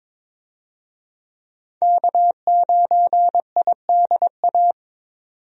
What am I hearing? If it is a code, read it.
Morse code, heard as a keyed tone, about 22 wpm, 700 Hz. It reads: K9IDA
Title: USA Call Signs - 5 Characters - 22wpm